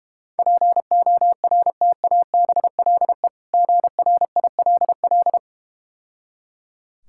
Morse code, heard as a keyed tone, about 31 wpm, 700 Hz. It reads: PORTABLE GRILL
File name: practice / 25wpm / Difficult Words - Speed-Racing with Hints - 25wpm